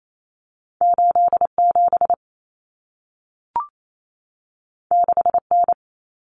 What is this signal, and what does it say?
Morse code, heard as a keyed tone, about 28 wpm, 700 Hz. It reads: 87 6N
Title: Common Cut Numbers - 2-Digit 28wpm